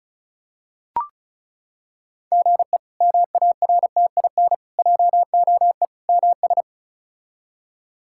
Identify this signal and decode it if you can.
Morse code, heard as a keyed tone, about 34 wpm, 700 Hz. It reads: GE MARTIN JOE MS
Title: K1USN SST QSO Practice - 35wpm